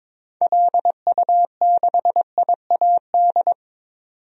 Morse code, heard as a keyed tone, about 22 wpm, 700 Hz. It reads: LU6IAD
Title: DXCC-100 Call Signs - 6 Characters - 22wpm